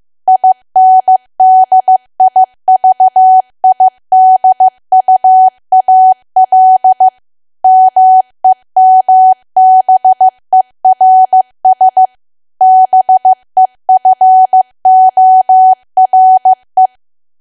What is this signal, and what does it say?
Morse code, heard as a keyed tone, about 15 wpm, 750 Hz. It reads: INDIVIDUAL MEMBERS BEFORE
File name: cwa / W1AW-Level2 / 15WPM